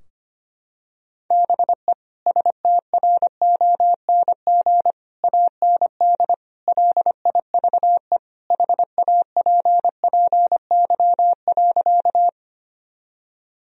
Morse code, heard as a keyed tone, about 25 wpm, 700 Hz. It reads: BE STRONG AND LIVE HAPPY.
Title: Sentences from Top 1000 Words - Review Part 3 - 25wpm